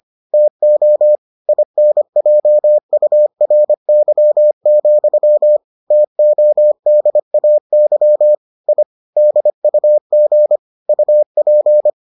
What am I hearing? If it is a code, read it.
Morse code, heard as a keyed tone, about 25 wpm, 600 Hz. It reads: TO INJURY, TODAY I DUG UP